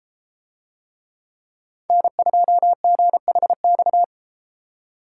Morse code, heard as a keyed tone, about 32 wpm, 700 Hz. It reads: N2GHX